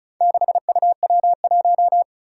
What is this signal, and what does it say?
Morse code, heard as a keyed tone, about 36 wpm, 700 Hz. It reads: 6UW1
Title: Four Challenging Letter-Number Combinations 35wpm